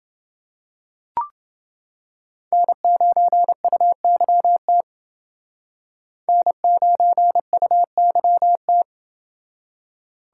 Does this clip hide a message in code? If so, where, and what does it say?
Yes: Morse code from 2.5 to 8.8 s, heard as a keyed tone, about 28 wpm, 700 Hz. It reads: N9UYT N9UYT